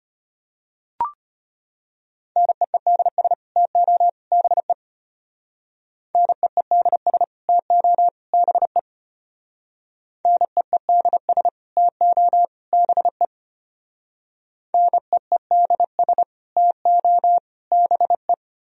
Morse code, heard as a keyed tone, about 31 wpm, 700 Hz. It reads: NEEDS TO BE NEEDS TO BE NEEDS TO BE NEEDS TO BE